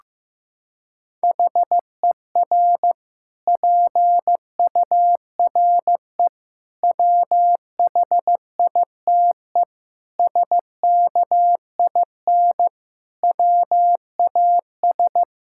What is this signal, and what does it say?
Morse code, heard as a keyed tone, about 15 wpm, 700 Hz. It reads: HER PURE WHITE SKIN WAS
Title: Sentences from Top 1600 Words 15wpm